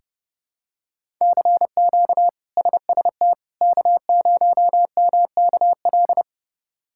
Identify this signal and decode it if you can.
Morse code, heard as a keyed tone, about 30 wpm, 700 Hz. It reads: CQ SST K0MKL
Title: K1USN SST QSO Practice - 30wpm